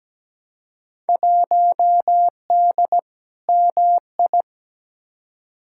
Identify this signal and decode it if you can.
Morse code, heard as a keyed tone, about 17 wpm, 700 Hz. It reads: 1D MI
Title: Field Day Exchanges - 17wpm